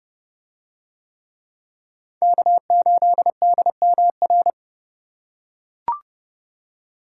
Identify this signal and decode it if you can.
Morse code, heard as a keyed tone, about 30 wpm, 700 Hz. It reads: K8DMR